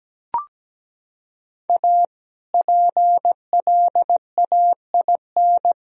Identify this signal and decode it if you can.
Morse code, heard as a keyed tone, about 17 wpm, 700 Hz. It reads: A PLAIN
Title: Sentences from Top 1000 Words - Review Part 2 - 17wpm